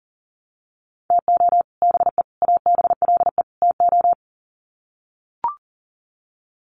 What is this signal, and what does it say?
Morse code, heard as a keyed tone, about 40 wpm, 700 Hz. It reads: TO BE ABLE TO
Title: Sets of 4 Words 40wpm